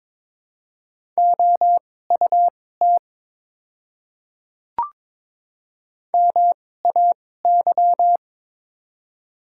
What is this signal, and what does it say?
Morse code, heard as a keyed tone, about 22 wpm, spaced as for 16 wpm, 700 Hz. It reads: OUT MAY
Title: 2x Character Spacing - Top 100 Words - 22wpm